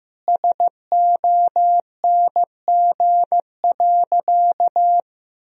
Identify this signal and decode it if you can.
Morse code, heard as a keyed tone, about 15 wpm, 700 Hz. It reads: SONG.
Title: Sentences from Top 1000 Words - Review Part 2 - 15wpm